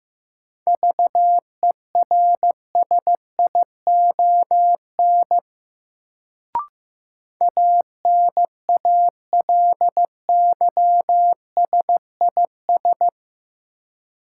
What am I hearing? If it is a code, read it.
Morse code, heard as a keyed tone, about 15 wpm, 700 Hz. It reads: VERSION ANALYSIS
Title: Sentences from Top 1400 Words - New Word Drill Alt - 15wpm